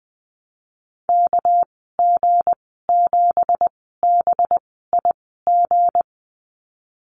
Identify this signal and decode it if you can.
Morse code, heard as a keyed tone, about 19 wpm, spaced as for 14 wpm, 700 Hz. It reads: KG7BIG
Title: 200 Lesson 32g - Letter k - Call Signs 20wpm